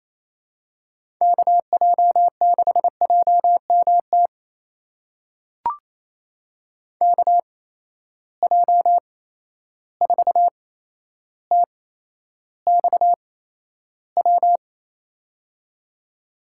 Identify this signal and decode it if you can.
Morse code, heard as a keyed tone, about 28 wpm, spaced as for 12 wpm, 700 Hz. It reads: KJ6JMT KJ4TXW
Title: USA Call Signs - 6 Characters - 8x Character Spacing - 28wpm